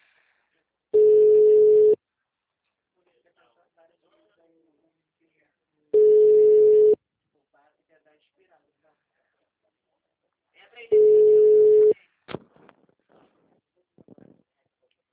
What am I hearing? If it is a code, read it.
Ringback tone (Europe, 425 Hz), with speech.
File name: out- -2017- - - .110